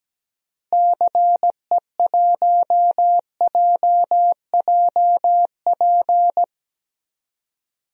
Morse code, heard as a keyed tone, about 17 wpm, 700 Hz. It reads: CE1JJP
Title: DXCC-100 Call Signs - 6 Characters - Repeated 2x - 17wpm